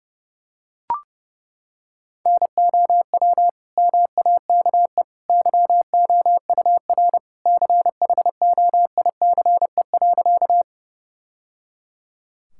Morse code, heard as a keyed tone, about 30 wpm, 700 Hz. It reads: NOW MAKE YOUR CHOICE.
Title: Sentences from Top 1000 Words - Review Part 6 - 30wpm